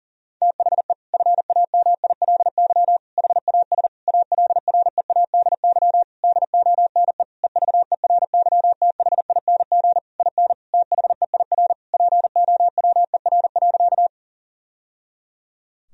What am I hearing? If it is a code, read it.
Morse code, heard as a keyed tone, about 37 wpm, 700 Hz. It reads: THE FAMILY HAS ALREADY DONE EVERYTHING IN THEIR POWER.
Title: Sentences from Top 1000 Words - Review Part 2 - 40wpm